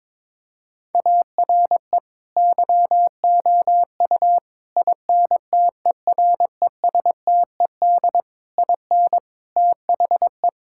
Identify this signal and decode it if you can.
Morse code, heard as a keyed tone, about 22 wpm, 700 Hz. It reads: ARE YOU INTERESTED IN THE
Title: Sentences from Top 1000 Words - Review Part 6 - 22wpm